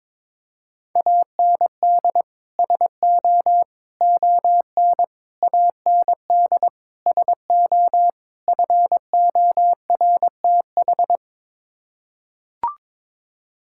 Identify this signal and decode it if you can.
Morse code, heard as a keyed tone, about 22 wpm, 700 Hz. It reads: AND SO ON AND SO FORTH
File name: practice / 22wpm / Sets of 6 Words - 22wpm